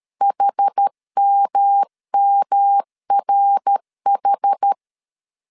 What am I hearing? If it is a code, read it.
Morse code, heard as a keyed tone, about 13 wpm, 800 Hz. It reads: HMMRH
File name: CW / DET / 1 3 1500C S B 12wpm 2828